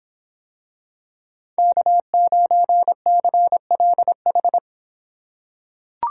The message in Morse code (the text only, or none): K9CLH